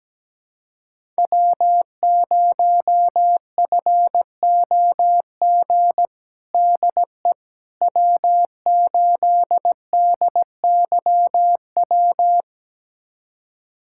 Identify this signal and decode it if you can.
Morse code, heard as a keyed tone, about 17 wpm, 700 Hz. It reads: W0FOG DE W8DYW
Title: USA Callsign DE Callsign - 5 Characters - 17wpm